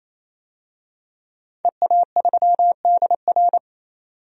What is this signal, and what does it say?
Morse code, heard as a keyed tone, about 27 wpm, 700 Hz. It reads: EA3DR
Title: DXCC-100 Call Signs - 5 Characters - 28wpm